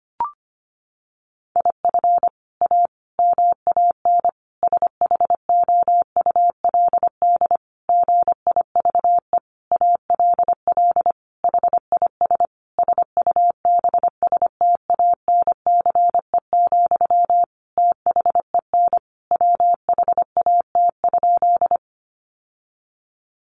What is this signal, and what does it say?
Morse code, heard as a keyed tone, about 25 wpm, 700 Hz. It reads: IF A MAN SHOULD GIVE ALL HIS SUBSTANCE, THEN WHAT?
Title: Sentences from Top 1500 Words 25wpm